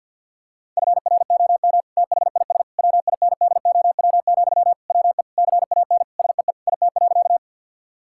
Morse code, heard as a keyed tone, about 50 wpm, 700 Hz. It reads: FROM THIS WINDOW, WE CAN SEE IT.
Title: Sentences from Top 1600 Words 50wpm